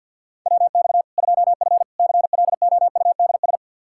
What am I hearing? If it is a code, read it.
Morse code, heard as a keyed tone, about 48 wpm, 700 Hz. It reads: WX 2F CLOUDS
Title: Practice QSO - Exchange 2 - Info - 50wpm